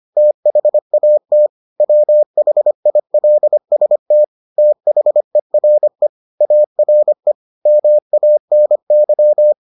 Morse code, heard as a keyed tone, about 25 wpm, 600 Hz. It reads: THAT WHILST THERE ARE MANY